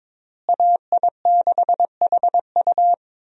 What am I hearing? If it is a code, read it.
Morse code, heard as a keyed tone, about 22 wpm, 700 Hz. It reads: AI6HU
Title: USA Call Signs - 5 Characters - Repeated 2x - 22wpm